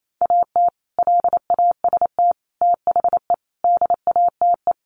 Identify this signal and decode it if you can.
Morse code, heard as a keyed tone, about 28 wpm, 700 Hz. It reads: AT LAST THE DATE
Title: Sentences from Top 1100 Words 28wpm